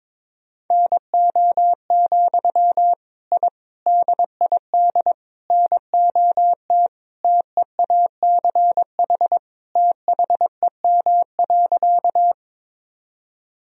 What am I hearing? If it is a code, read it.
Morse code, heard as a keyed tone, about 22 wpm, 700 Hz. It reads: NO, I DID NOT TEACH THEM.
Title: Sentences from Top 1000 Words - Review Part 2 - 22wpm